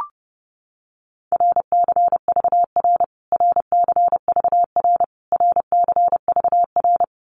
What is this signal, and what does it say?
Morse code, heard as a keyed tone, about 30 wpm, 700 Hz. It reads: RCVR RCVR RCVR
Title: Abbreviations Repeated 3x - 30wpm